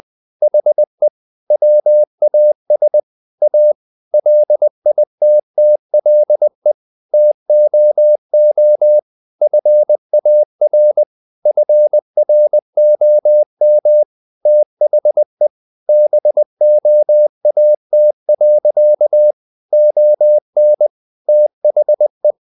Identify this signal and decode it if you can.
Morse code, heard as a keyed tone, about 20 wpm, 600 Hz. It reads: HE WAS A LITTLE TOO FAR FROM THE BOAT. ON THE
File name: the old man and the sea part 2 20wpm